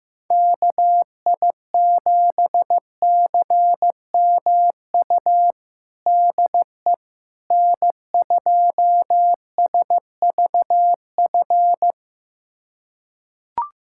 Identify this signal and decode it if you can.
Morse code, heard as a keyed tone, about 15 wpm, 700 Hz. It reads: KI7CMU DE N2SVF
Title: USA Callsign DE Callsign 15wpm